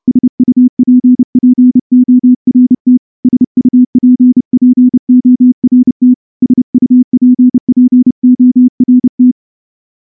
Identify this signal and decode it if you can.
Morse code, heard as a keyed tone, about 30 wpm, 250 Hz. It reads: SUPPORT SUPPORT SUPPORT
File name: morse-training-30WPM